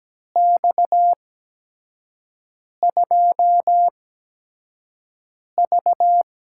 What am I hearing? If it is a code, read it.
Morse code, heard as a keyed tone, about 17 wpm, spaced as for 5 wpm, 700 Hz. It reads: X2V
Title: USA Call Signs - 4 Characters - 8x Character Spacing - 17wpm